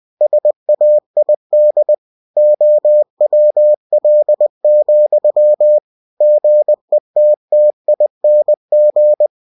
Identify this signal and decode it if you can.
Morse code, heard as a keyed tone, about 20 wpm, 600 Hz. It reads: SAID OWL, GETTING
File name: cw-books / winnie-the-pooh / winnie-20wpm-page-0105